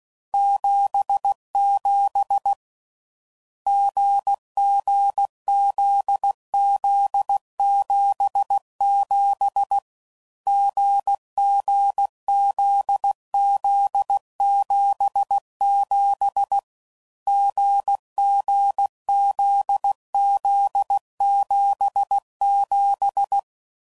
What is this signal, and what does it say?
Morse code, heard as a keyed tone, about 16 wpm, 800 Hz. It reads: 77 GG ZZ 77 GG ZZ 77 GG ZZ 77
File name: Combinations / Difficult Combos / GZ GZ7